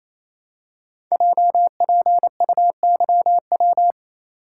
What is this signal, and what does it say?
Morse code, heard as a keyed tone, about 27 wpm, 700 Hz. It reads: JPUYW
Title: Code Groups - 5 Letters - 28wpm